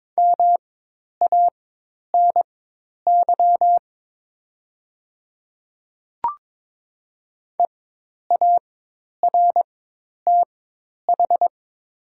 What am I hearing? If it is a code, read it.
Morse code, heard as a keyed tone, about 22 wpm, spaced as for 10 wpm, 700 Hz. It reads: MANY EARTH
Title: 4x Character Spacing - Top 200 Words - 22wpm